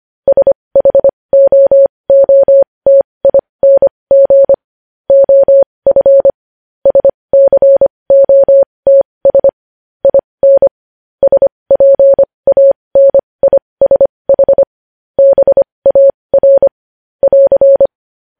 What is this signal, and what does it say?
Morse code, heard as a keyed tone, about 25 wpm, spaced as for 20 wpm, 550 Hz. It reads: SHOOTING OF SCOTS IN SPANISH BAR <AR>
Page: news headlines in morse code at 20 wpm